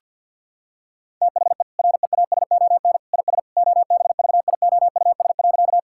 Morse code, heard as a keyed tone, about 52 wpm, 700 Hz. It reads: THE REASON IS OBVIOUS.
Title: Sentences from Top 1600 Words 50wpm